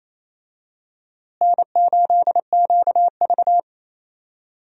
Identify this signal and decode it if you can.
Morse code, heard as a keyed tone, about 28 wpm, 700 Hz. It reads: N8QV
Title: USA Call Signs - 4 Characters - Repeated 2x - 28wpm